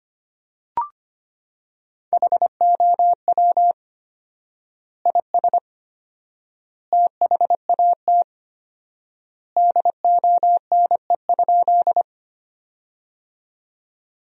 Morse code, heard as a keyed tone, about 25 wpm, 700 Hz. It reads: HOW IS THAT DONE?
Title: 4x Word-Spacing - Sentences from Top 500 Words - 25wpm